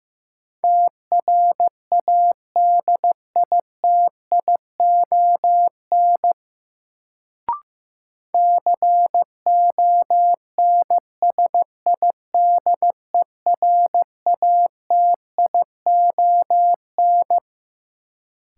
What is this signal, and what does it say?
Morse code, heard as a keyed tone, about 15 wpm, 700 Hz. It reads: TRADITION CONSIDERATION
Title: Sentences from Top 1500 Words - New Word Drill - 15wpm